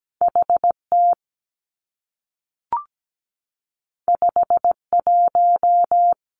Morse code, heard as a keyed tone, about 17 wpm, 700 Hz. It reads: HT 51
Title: Two Challenging Letter-Number Combinations 17wpm